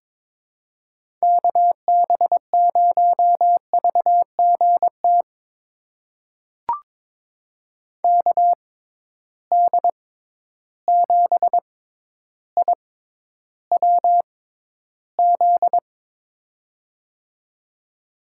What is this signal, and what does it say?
Morse code, heard as a keyed tone, about 22 wpm, spaced as for 11 wpm, 700 Hz. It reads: KB0VGT KD7IWZ
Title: USA Call Signs - 6 Characters - 6x Character Spacing - 22wpm